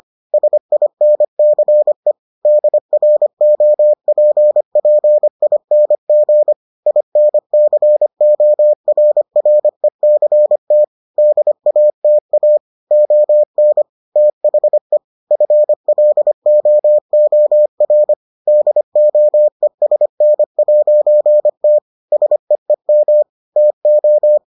Morse code, heard as a keyed tone, about 25 wpm, 600 Hz. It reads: SINCE DROPPING INCORRECT DATA ON THE FLOOR DOESN'T SEEM TO